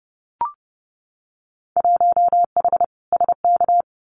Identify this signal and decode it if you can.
Morse code, heard as a keyed tone, about 30 wpm, 700 Hz. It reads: 1H SK
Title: Winter Field Day Exchanges 30wpm